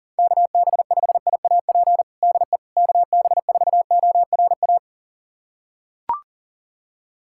Morse code, heard as a keyed tone, about 40 wpm, 690 Hz. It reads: KB5IAP DE KB4ORA
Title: USA Callsign DE Callsign - 6 Characters - 40wpm